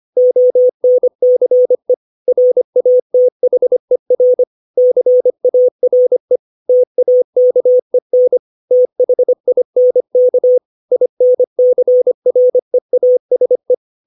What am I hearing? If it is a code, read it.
Morse code, heard as a keyed tone, about 25 wpm, 500 Hz. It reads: ONCE RATHER CARE TAKEN THINK INCREASE
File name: 632 words / 632 Words 25WPM